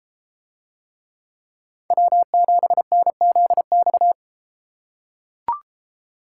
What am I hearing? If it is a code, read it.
Morse code, heard as a keyed tone, about 32 wpm, 700 Hz. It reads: W7NZX